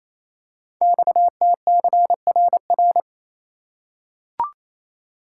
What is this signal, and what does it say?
Morse code, heard as a keyed tone, about 27 wpm, 700 Hz. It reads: XTCRR
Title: Code Groups - 5 Letters - 28wpm